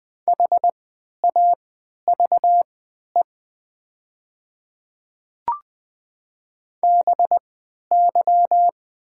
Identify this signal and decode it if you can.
Morse code, heard as a keyed tone, about 20 wpm, spaced as for 11 wpm, 700 Hz. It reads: HAVE BY
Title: 3x Character Spacing - Top 100 Words - 20wpm